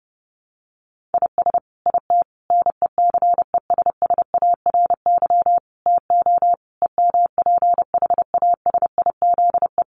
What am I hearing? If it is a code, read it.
Morse code, heard as a keyed tone, about 30 wpm, 700 Hz. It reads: IS IT NECESSARY TO EMPHASIZE
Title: Sentences from Top 1500 Words 30wpm